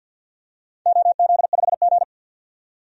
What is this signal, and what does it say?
Morse code, heard as a keyed tone, about 52 wpm, 700 Hz. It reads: OZ5G
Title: DXCC-100 Call Signs - 4 Characters - 50wpm